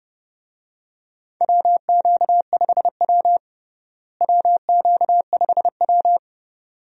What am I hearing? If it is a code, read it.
Morse code, heard as a keyed tone, about 30 wpm, 700 Hz. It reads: WQ5W WQ5W
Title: USA Call Signs - 4 Characters - Repeated 2x - 30wpm